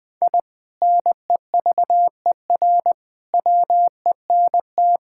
Morse code, heard as a keyed tone, about 20 wpm, 700 Hz. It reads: I NEVER WENT
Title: Sentences from Top 1000 Words - Review Part 5 - 20wpm